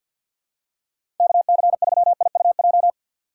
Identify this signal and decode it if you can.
Morse code, heard as a keyed tone, about 53 wpm, 700 Hz. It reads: KC3IUJ
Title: USA Call Signs - 4x Character Spacing - 50wpm